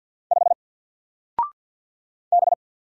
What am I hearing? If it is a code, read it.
Morse code, heard as a keyed tone, about 49 wpm, 700 Hz. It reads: 5 B
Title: Single Character (Letter:Number:Punctuation:Prosign) Alt 50wpm